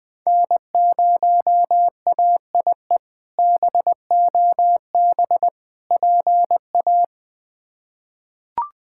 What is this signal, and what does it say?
Morse code, heard as a keyed tone, about 20 wpm, 700 Hz. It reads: N0AIE BOB PA
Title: K1USN SST QSO Practice - 20wpm